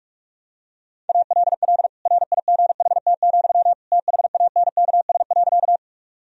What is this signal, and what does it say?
Morse code, heard as a keyed tone, about 46 wpm, 700 Hz. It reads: ALL RIGHT, THANKS.
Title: Sentences from Top 1000 Words - Review Part 3 - 45wpm